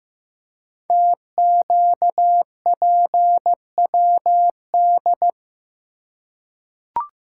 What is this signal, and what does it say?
Morse code, heard as a keyed tone, about 15 wpm, 700 Hz. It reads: TQPWD
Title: Code Groups - 5 Letters - 15wpm